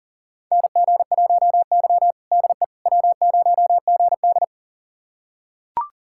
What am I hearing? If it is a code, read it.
Morse code, heard as a keyed tone, about 40 wpm, 700 Hz. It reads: NG1Y DE W0GD
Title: USA Callsign DE Callsign - 4 Characters - 40wpm